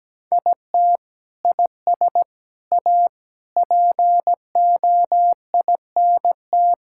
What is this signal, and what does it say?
Morse code, heard as a keyed tone, about 17 wpm, 700 Hz. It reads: IT IS A POINT
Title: Sentences from Top 1000 Words - Review Part 5 - 17wpm